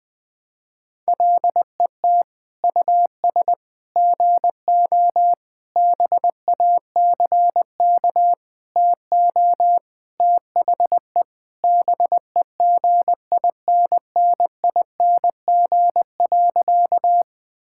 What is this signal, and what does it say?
Morse code, heard as a keyed tone, about 20 wpm, 700 Hz. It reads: LET US GO BACK TO THE BEGINNING.
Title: Sentences with 4-Word Phrases - Spoken Beginning - 20wpm